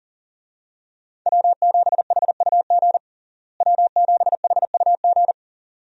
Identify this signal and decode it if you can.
Morse code, heard as a keyed tone, about 40 wpm, 700 Hz. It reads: W7HUG W7HUG
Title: USA Call Signs - 5 Characters - Repeated 2x - 40wpm